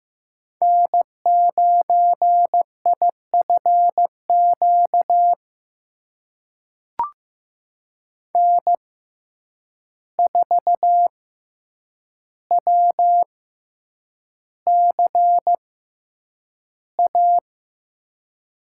Morse code, heard as a keyed tone, about 16 wpm, spaced as for 8 wpm, 700 Hz. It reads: N9IFQ N4WCA
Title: USA Call Signs - 5 Characters - 6x Character Spacing - 15wpm